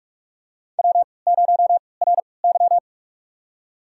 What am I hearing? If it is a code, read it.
Morse code, heard as a keyed tone, about 46 wpm, spaced as for 25 wpm, 700 Hz. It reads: W0RY
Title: USA Call Signs - 4 Characters - 3x Character Spacing - 45wpm